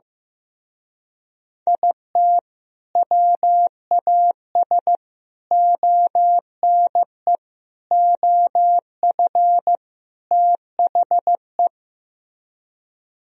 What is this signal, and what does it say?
Morse code, heard as a keyed tone, about 15 wpm, 700 Hz. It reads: IT WAS ONE OF THE
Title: Speed-Racing - Sets of 5 Words 15wpm